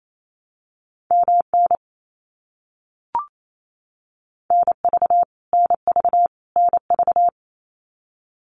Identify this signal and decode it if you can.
Morse code, heard as a keyed tone, about 28 wpm, 700 Hz. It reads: MN NV NV NV
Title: US States Repeated 3x 28wpm